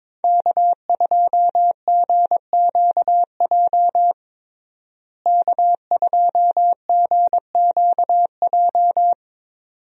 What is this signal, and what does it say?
Morse code, heard as a keyed tone, about 22 wpm, 700 Hz. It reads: K2GQJ K2GQJ
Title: USA Call Signs - 5 Characters - Repeated 2x - 22wpm